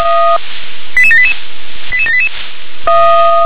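Five-tone selective call (ZVEI), heard as quick runs of five tones, tones of about 70 ms. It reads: 88788 88788
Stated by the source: ZVEI five-tone selective call received by SDR.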